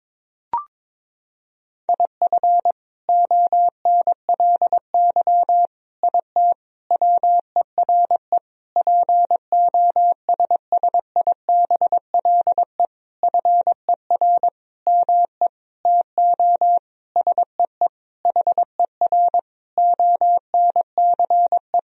Morse code, heard as a keyed tone, about 21 wpm, 700 Hz. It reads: IF ONLY IT WERE POSSIBLE FER ME TO SEE HER ONCE
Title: Sentences from Top 500 Words and Abbreviations 22wpm